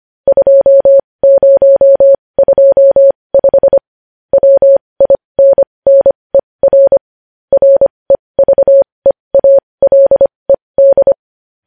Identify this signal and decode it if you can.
Morse code, heard as a keyed tone, about 25 wpm, spaced as for 20 wpm, 550 Hz. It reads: 2025 WINNER REVEALED
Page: news headlines in morse code at 20 wpm